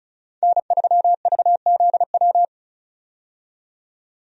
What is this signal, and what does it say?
Morse code, heard as a keyed tone, about 36 wpm, 700 Hz. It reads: N3VZW
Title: USA Call Signs - 5 Characters - 35wpm